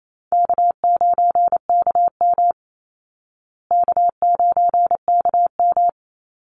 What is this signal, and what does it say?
Morse code, heard as a keyed tone, about 28 wpm, 700 Hz. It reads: K9KM K9KM
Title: Easy Call Signs Repeated 2x - 28wpm